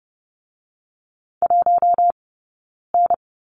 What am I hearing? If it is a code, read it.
Morse code, heard as a keyed tone, about 30 wpm, 700 Hz. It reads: 1 N
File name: Lesson 16d - Letter c - Two Characters 30wpm